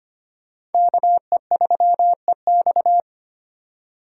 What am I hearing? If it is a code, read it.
Morse code, heard as a keyed tone, about 32 wpm, 700 Hz. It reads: KE3EX